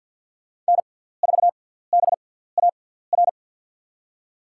Morse code, heard as a keyed tone, about 49 wpm, spaced as for 17 wpm, 700 Hz. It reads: N4BAR
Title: USA Call Signs - 6x Character Spacing - 50wpm